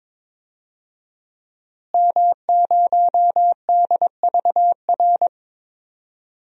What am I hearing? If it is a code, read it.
Morse code, heard as a keyed tone, about 22 wpm, 700 Hz. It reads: M0DVR
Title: DXCC-100 Call Signs - 5 Characters - Repeated 2x - 22wpm